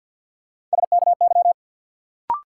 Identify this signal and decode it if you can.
Morse code, heard as a keyed tone, about 46 wpm, 700 Hz. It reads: SKY
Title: Sentences from Top 1200 Words - New Word Drill - 50wpm